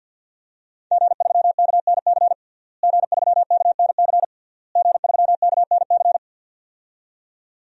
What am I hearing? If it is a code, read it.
Morse code, heard as a keyed tone, about 50 wpm, 700 Hz. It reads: G3KNC G3KNC G3KNC
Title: DXCC-100 Call Signs - 5 Characters - Repeated 3x - 50wpm